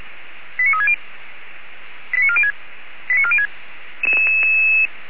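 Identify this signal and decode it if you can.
Five-tone selective call (ZVEI), heard as quick runs of five tones, tones of about 70 ms. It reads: 89360 89406 89406
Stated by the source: ZVEI five-tone selective call 89386 received on an SDR.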